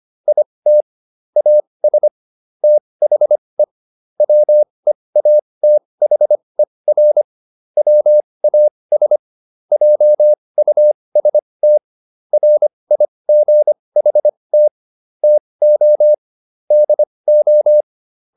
Morse code, heard as a keyed tone, about 25 wpm, spaced as for 20 wpm, 600 Hz. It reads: IT AS THE WEATHER WAS JUST RIGHT TO DO